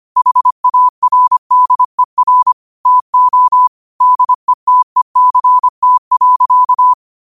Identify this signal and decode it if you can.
Morse code, heard as a keyed tone, about 25 wpm, 1000 Hz. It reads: HARDER TO DETECT.
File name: qotdfiles / 25wpm